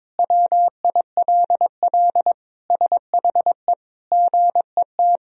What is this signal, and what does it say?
Morse code, heard as a keyed tone, about 22 wpm, 700 Hz. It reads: WILL SHE GET
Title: Sentences from Top 1000 Words - Review Part 1 - 22wpm